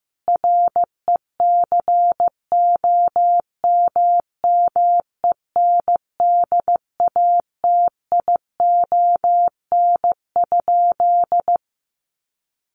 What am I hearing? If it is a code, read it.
Morse code, heard as a keyed tone, about 15 wpm, 700 Hz. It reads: RECOMMENDATION?
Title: Sentences from Top 1400 Words 15wpm